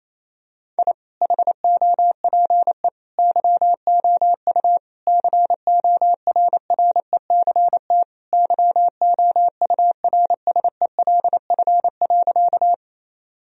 Morse code, heard as a keyed tone, about 28 wpm, 700 Hz. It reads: I HOPE YOU CORRECT YOURSELF.
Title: Sentences from Top 1000 Words - Review Part 2 - 28wpm